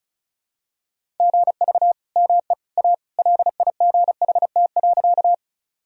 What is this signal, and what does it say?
Morse code, heard as a keyed tone, about 36 wpm, 700 Hz. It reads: GV ME A LIGHT.
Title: Sentences from Top 200 Words and Abbreviations 35wpm